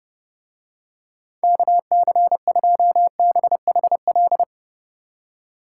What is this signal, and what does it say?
Morse code, heard as a keyed tone, about 31 wpm, 700 Hz. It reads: KC2BHL